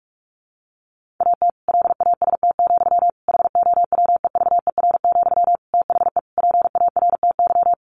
Morse code, heard as a keyed tone, about 44 wpm, 700 Hz. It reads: AT LAST, HOWEVER, THE PARTY
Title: Sentences from Top 1400 Words 45wpm